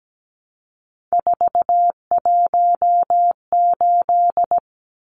Morse code, heard as a keyed tone, about 17 wpm, 700 Hz. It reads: 418
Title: Numbers - 3-Digit 17wpm